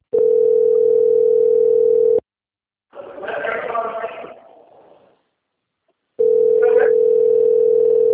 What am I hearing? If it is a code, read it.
Ringback tone (North America), 440 + 480 Hz, with speech.
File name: out- - - - - .18044